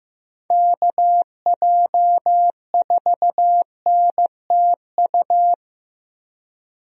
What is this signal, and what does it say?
Morse code, heard as a keyed tone, about 15 wpm, 700 Hz. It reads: KJ4NTU
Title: USA Call Signs - 6 Characters - Repeated 3x - 15wpm